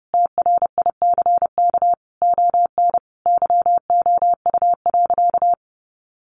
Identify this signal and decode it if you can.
Morse code, heard as a keyed tone, about 30 wpm, 700 Hz. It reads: TRICK ON YOU.
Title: Sentences from Top 1500 Words 30wpm